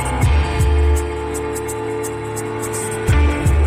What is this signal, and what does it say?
Dial tone (North America), 350 + 440 Hz, with noise.